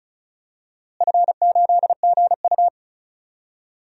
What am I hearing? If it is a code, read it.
Morse code, heard as a keyed tone, about 34 wpm, 700 Hz. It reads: F8GU
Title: DXCC-100 Call Signs - 4 Characters - Repeated 3x - 35wpm